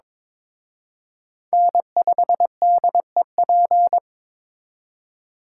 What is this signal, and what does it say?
Morse code, heard as a keyed tone, about 26 wpm, 700 Hz. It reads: N5DEP
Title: Speed Racing - USA 1x3 Call Signs 22wpm